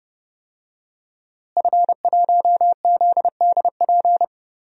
Morse code, heard as a keyed tone, about 28 wpm, 700 Hz. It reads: F1ZDP
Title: DXCC-100 Call Signs - 5 Characters - Repeated 3x - 30wpm